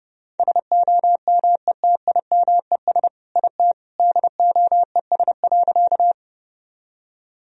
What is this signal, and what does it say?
Morse code, heard as a keyed tone, about 32 wpm, 700 Hz. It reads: SOMETIMES IT DOES.
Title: Sentences from Top 1000 Words - Review Part 3 - 30wpm